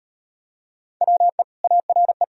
Morse code, heard as a keyed tone, about 39 wpm, 700 Hz. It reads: WE ARE